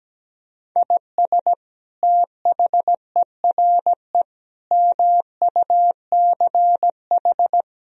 Morse code, heard as a keyed tone, about 17 wpm, 700 Hz. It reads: IS THERE MUCH
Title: Sentences from Top 1000 Words - Review Part 4 - 17wpm